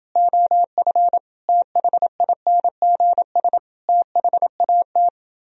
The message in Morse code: OF THINGS THAT

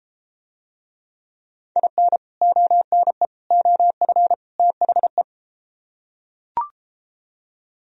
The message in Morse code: IN ONE OF THE

